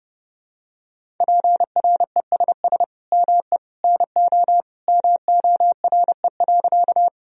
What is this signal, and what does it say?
Morse code, heard as a keyed tone, about 30 wpm, 700 Hz. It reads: PRESS ME NO MORE.
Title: Sentences from Top 1000 Words - Review Part 1 - 30wpm